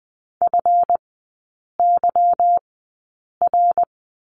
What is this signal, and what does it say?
Morse code, heard as a keyed tone, about 20 wpm, spaced as for 8 wpm, 690 Hz. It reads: FYR
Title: 145 Lesson 24e - Letter y - Three Characters 20wpm